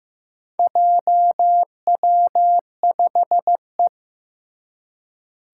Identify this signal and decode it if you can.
Morse code, heard as a keyed tone, about 15 wpm, 700 Hz. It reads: JW5E
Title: Four Challenging Letter-Number Combinations 15wpm